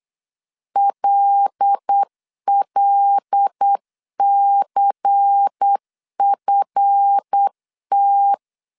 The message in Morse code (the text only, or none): LLCFT